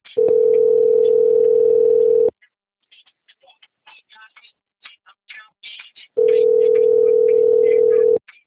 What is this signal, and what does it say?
Ringback tone (North America), 440 + 480 Hz.